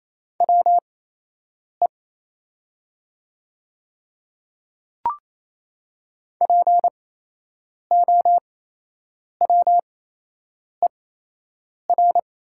Morse code, heard as a keyed tone, about 27 wpm, spaced as for 8 wpm, 700 Hz. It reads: WE POWER
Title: 8x Character Spacing - Top 500 Words - 28wpm